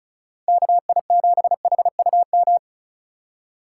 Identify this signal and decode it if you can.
Morse code, heard as a keyed tone, about 47 wpm, spaced as for 41 wpm, 700 Hz. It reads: KI7HUM
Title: Speed Racing - USA 2x3 Call Signs 35wpm